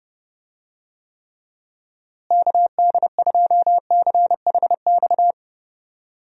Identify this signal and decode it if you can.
Morse code, heard as a keyed tone, about 32 wpm, 700 Hz. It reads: KD2CHX
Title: USA Call Signs - 6 Characters - Repeated 2x - 30wpm